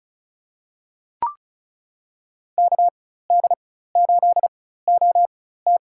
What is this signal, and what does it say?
Morse code, heard as a keyed tone, about 36 wpm, spaced as for 16 wpm, 700 Hz. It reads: KD8OT
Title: USA Call Signs - 5 Characters - 4x Character Spacing - 35wpm